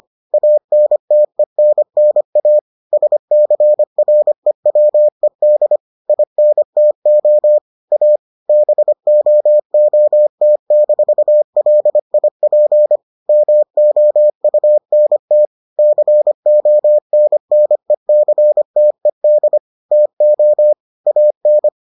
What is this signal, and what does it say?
Morse code, heard as a keyed tone, about 25 wpm, 600 Hz. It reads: ANTENNA SCREWED INTO A BOOT-LIP MOUNT CONNECTED TO AN